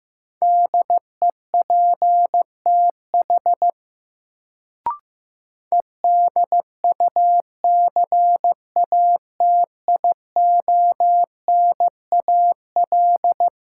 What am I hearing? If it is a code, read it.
Morse code, heard as a keyed tone, about 15 wpm, 700 Hz. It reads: DEPTH EDUCATIONAL
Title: Sentences from Top 1200 Words - New Word Drill - 15wpm